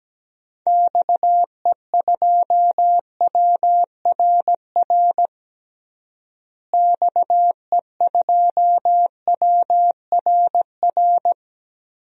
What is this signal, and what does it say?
Morse code, heard as a keyed tone, about 17 wpm, 700 Hz. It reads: XE2WRR XE2WRR
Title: DXCC-100 Call Signs - 6 Characters - Repeated 2x - 17wpm